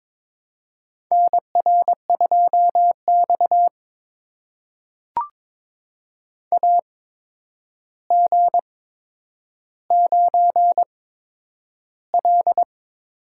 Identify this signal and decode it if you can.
Morse code, heard as a keyed tone, about 22 wpm, spaced as for 9 wpm, 700 Hz. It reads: NR2X AG9L
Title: USA Call Signs - 4 Characters - 8x Character Spacing - 22wpm